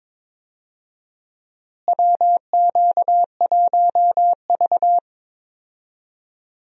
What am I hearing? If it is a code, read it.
Morse code, heard as a keyed tone, about 22 wpm, 700 Hz. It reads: WQ1V
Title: USA Call Signs - 4 Characters - Repeated 3x - 22wpm